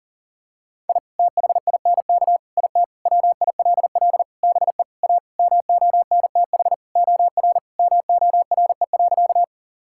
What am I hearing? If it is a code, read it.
Morse code, heard as a keyed tone, about 40 wpm, 700 Hz. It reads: I THINK IT WILL BE A MONTH OR MORE.
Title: Sentences from Top 1000 Words - Review Part 5 - 40wpm